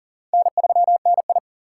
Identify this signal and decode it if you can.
Morse code, heard as a keyed tone, about 40 wpm, 690 Hz. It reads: N3NI
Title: USA Call Signs - 4 Characters - 6x Character Spacing - 40wpm